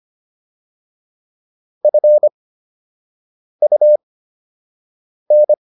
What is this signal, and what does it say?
Morse code, heard as a keyed tone, about 25 wpm, spaced as for 6 wpm, 600 Hz. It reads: FUN